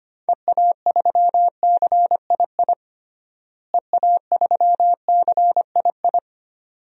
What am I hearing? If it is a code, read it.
Morse code, heard as a keyed tone, about 25 wpm, 700 Hz. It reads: EA3CII EA3CII
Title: DXCC-100 Call Signs - 6 Characters - Repeated 2x - 25wpm